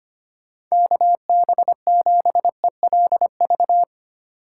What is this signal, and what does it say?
Morse code, heard as a keyed tone, about 25 wpm, 700 Hz. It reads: KB7ELV
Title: USA Call Signs - 6 Characters - 2x Character Spacing - 25wpm